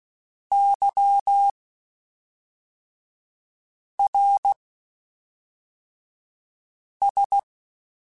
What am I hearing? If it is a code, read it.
Morse code, heard as a keyed tone, about 16 wpm, spaced as for 3 wpm, 800 Hz. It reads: YRS
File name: (spaced) / QSOs Spaced-mp3s / kc7sac @5 sp